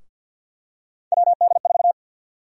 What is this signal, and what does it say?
Morse code, heard as a keyed tone, about 53 wpm, 700 Hz. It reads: WD4
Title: USA Call Sign Prefixes - Repeated 2x - 50wpm